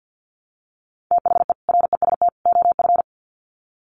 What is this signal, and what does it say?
Morse code, heard as a keyed tone, about 50 wpm, 700 Hz. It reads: THE REST OF
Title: Sets of 3 Words 50wpm